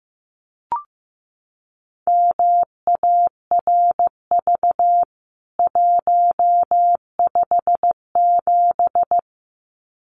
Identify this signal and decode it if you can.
Morse code, heard as a keyed tone, about 15 wpm, 700 Hz. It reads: MARV 157
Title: CWT Exchange 15wpm